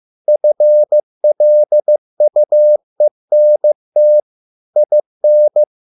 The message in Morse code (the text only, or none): FLUENT IN